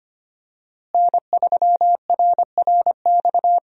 Morse code, heard as a keyed tone, about 25 wpm, 700 Hz. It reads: N3RRX
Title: USA Call Signs - 5 Characters - Repeated 2x - 25wpm